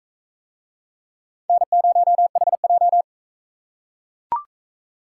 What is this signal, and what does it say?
Morse code, heard as a keyed tone, about 37 wpm, 700 Hz. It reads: N0HJ